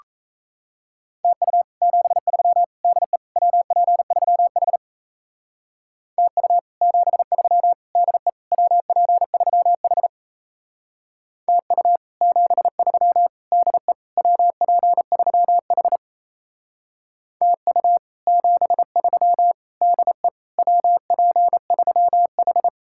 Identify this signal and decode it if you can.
Morse code, heard as a keyed tone, about 34 wpm, 700 Hz. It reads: TU 73 DE WP3H TU 73 DE WP3H TU 73 DE WP3H TU 73 DE WP3H
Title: Speed-Racing Common QSO Phrases Encore - 28wpm